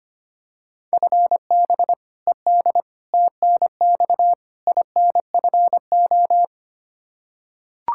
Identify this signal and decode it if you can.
Morse code, heard as a keyed tone, about 25 wpm, 700 Hz. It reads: FB ED TNX INFO
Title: Practice QSO - Exchange 3 - More Info - 25wpm